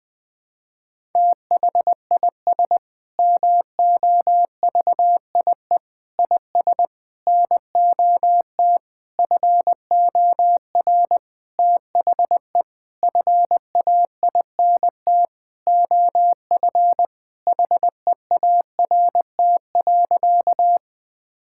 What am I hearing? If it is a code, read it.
Morse code, heard as a keyed tone, about 21 wpm, 700 Hz. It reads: THIS MOVIE IS NOT FOR THE FAINT OF HEART.
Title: Sentences with 6-Word Phrases - 20wpm